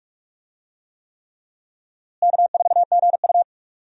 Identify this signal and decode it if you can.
Morse code, heard as a keyed tone, about 46 wpm, 700 Hz. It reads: K4GU
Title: USA Call Signs - 4 Characters - Repeated 3x - 45wpm